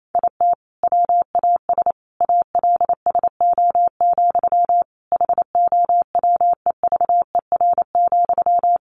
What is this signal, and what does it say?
Morse code, heard as a keyed tone, about 28 wpm, 700 Hz. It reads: IT WAS ALSO, HOWEVER,
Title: Sentences from Top 1400 Words 28wpm